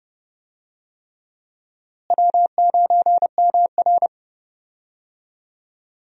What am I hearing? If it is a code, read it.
Morse code, heard as a keyed tone, about 30 wpm, 700 Hz. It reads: W9MR
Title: USA Call Signs - 4 Characters - Repeated 3x - 30wpm